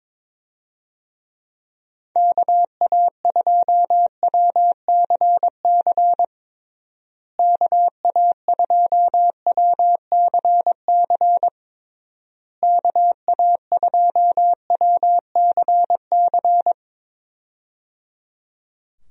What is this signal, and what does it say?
Morse code, heard as a keyed tone, about 22 wpm, 700 Hz. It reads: KA2WCC KA2WCC KA2WCC
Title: USA Call Signs - 6 Characters - Repeated 3x - 22wpm